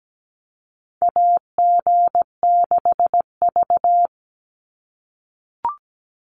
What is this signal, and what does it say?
Morse code, heard as a keyed tone, about 17 wpm, 700 Hz. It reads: AG6V
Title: Call Signs - 17wpm